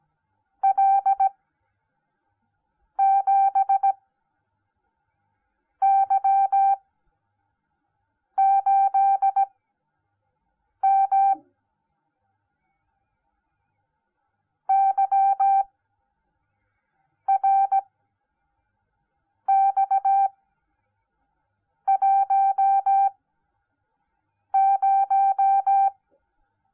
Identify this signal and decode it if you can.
Morse code, heard as a keyed tone, about 17 wpm, spaced as for 5 wpm, 780 Hz. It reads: L7Y8M YRX10